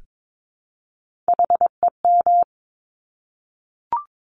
Morse code, heard as a keyed tone, about 22 wpm, 700 Hz. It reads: HEM
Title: Top Three Letter Combinations in Words 22wpm